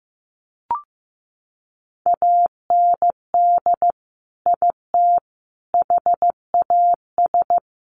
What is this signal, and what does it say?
Morse code, heard as a keyed tone, about 15 wpm, 700 Hz. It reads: AND IT HAS
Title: Sentences from Top 1300 Words 15wpm